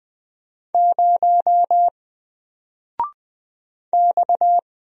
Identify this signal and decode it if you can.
Morse code, heard as a keyed tone, about 20 wpm, 700 Hz. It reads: 0 X
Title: Single Character (Letter:Number:Punctuation:Prosign) 20wpm